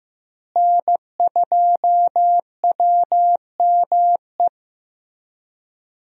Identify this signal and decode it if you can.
Morse code, heard as a keyed tone, about 15 wpm, 700 Hz. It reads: N2WME
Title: USA Call Signs - 5 Characters - Repeated 2x - 15wpm